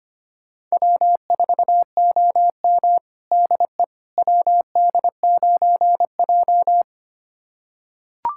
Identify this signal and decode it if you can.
Morse code, heard as a keyed tone, about 25 wpm, 700 Hz. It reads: W4OM DE WD9J
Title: USA Callsign DE Callsign - 4 Characters - 25wpm